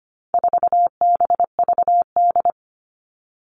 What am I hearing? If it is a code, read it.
Morse code, heard as a keyed tone, about 25 wpm, 700 Hz. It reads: 4BVD
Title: All Cut Numbers - 4-Digit 25wpm